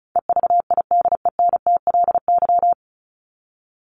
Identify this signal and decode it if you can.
Morse code, heard as a keyed tone, about 34 wpm, 700 Hz. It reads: EVIDENTLY
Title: LY Suffix - Top 100 Words with Base Words 35wpm